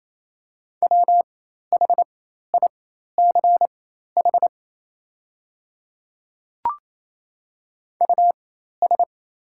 Morse code, heard as a keyed tone, about 28 wpm, spaced as for 13 wpm, 690 Hz. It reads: WHICH US
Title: 4x Character Spacing - Top 200 Words - 28wpm